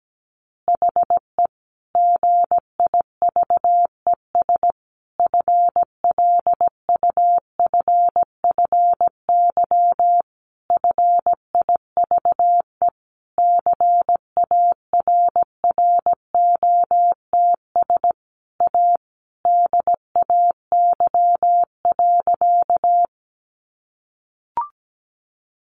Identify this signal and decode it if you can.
Morse code, heard as a keyed tone, about 17 wpm, 700 Hz. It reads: HE GIVES FLUFFY FIVE CARROTS A DAY.
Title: Easy Phrases 17wpm